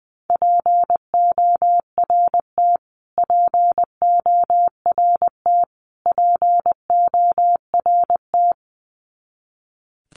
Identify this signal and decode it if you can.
Morse code, heard as a keyed tone, about 20 wpm, 700 Hz. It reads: PORT PORT PORT
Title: Top 300 Words Repeated 20wpm